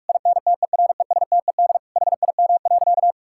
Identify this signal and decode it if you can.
Morse code, heard as a keyed tone, about 46 wpm, 700 Hz. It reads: INTERESTED HIM.
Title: Sentences from Top 1000 Words - Review Part 6 - 45wpm